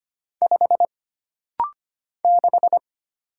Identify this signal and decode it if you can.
Morse code, heard as a keyed tone, about 25 wpm, 700 Hz. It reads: 5 6
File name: Single Character (Letter:Number:Punctuation:Prosign) - Rapid-Fire - 25wpm